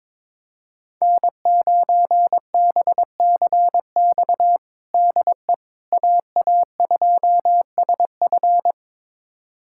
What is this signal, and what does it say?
Morse code, heard as a keyed tone, about 21 wpm, 700 Hz. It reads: N9BCX DE AA2SF
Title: USA Callsign DE Callsign - 5 Characters - 22wpm